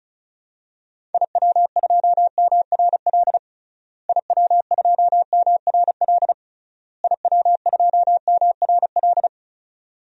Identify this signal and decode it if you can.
Morse code, heard as a keyed tone, about 36 wpm, 700 Hz. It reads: IW2MRL IW2MRL IW2MRL
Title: DXCC-100 Call Signs - 6 Characters - Repeated 3x - 35wpm